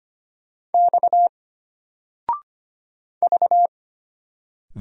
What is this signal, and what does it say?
Morse code, heard as a keyed tone, about 25 wpm, 700 Hz. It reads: X V
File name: Single-Letter-Number-25wpm